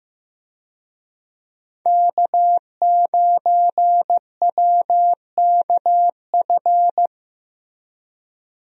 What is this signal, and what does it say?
Morse code, heard as a keyed tone, about 15 wpm, 700 Hz. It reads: K9WKF
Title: USA Call Signs - 5 Characters - 15wpm